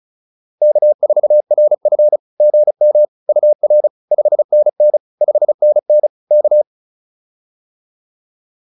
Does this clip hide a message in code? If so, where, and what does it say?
Yes: Morse code from 0.6 to 6.6 s, heard as a keyed tone, about 34 wpm, 600 Hz. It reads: K4RF GM UR 5NN 5NN K